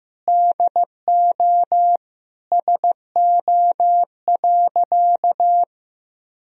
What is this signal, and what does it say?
Morse code, heard as a keyed tone, about 15 wpm, 700 Hz. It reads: DO SO.
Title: Sentences from Top 1000 Words - Review Part 1 - 15wpm